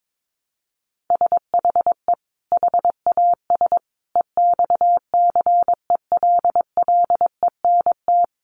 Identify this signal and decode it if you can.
Morse code, heard as a keyed tone, about 22 wpm, 700 Hz. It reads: SHE HAS EXCELLENT
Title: Sentences from Top 1100 Words 22wpm